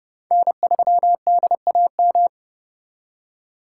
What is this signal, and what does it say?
Morse code, heard as a keyed tone, about 39 wpm, 700 Hz. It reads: N3DAM